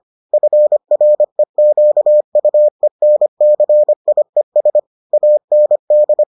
Morse code, heard as a keyed tone, about 25 wpm, 600 Hz. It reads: FREQUENCIES AND